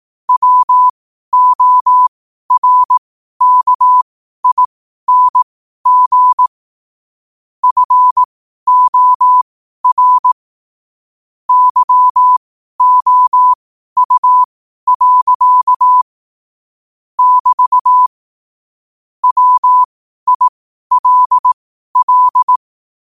Morse code, heard as a keyed tone, about 18 wpm, spaced as for 12 wpm, 1000 Hz. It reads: WORKING FOR YOU. <BT> WILL
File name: qotdfiles / 12wpm